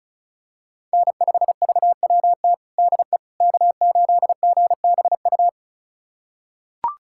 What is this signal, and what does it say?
Morse code, heard as a keyed tone, about 36 wpm, 700 Hz. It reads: N5VWT DE K8GBU
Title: USA Callsign DE Callsign - 5 Characters - 35wpm